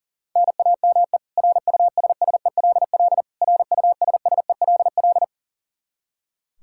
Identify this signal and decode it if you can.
Morse code, heard as a keyed tone, about 37 wpm, 700 Hz. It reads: NAME RUSSELL RUSSELL
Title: K1USN SST QSO Practice 40wpm